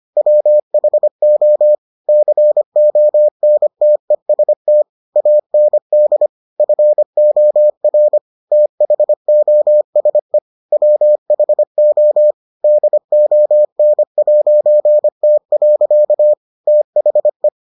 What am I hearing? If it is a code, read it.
Morse code, heard as a keyed tone, about 25 wpm, 600 Hz. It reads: WHO CONTEST AND FOR THOSE WHO DON'T. THE